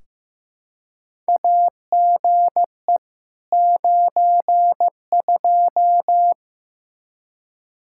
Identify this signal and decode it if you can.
Morse code, heard as a keyed tone, about 15 wpm, 700 Hz. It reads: AGE 92
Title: Practice QSO - Exchange 3 - More Info - 15wpm